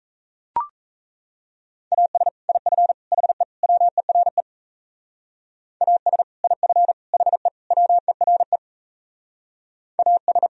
Morse code, heard as a keyed tone, about 37 wpm, 700 Hz. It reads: AS IF HE WERE AS IF HE WERE AS